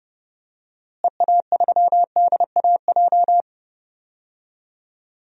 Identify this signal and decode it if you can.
Morse code, heard as a keyed tone, about 32 wpm, 700 Hz. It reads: EA3DAJ
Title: DXCC-100 Call Signs - 6 Characters - Repeated 3x - 30wpm